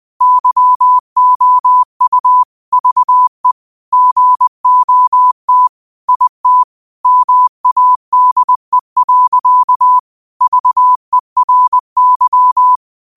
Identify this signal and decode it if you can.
Morse code, heard as a keyed tone, about 20 wpm, 1000 Hz. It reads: YOU VE GOT IT MADE. VERY
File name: qotdfiles / 20wpm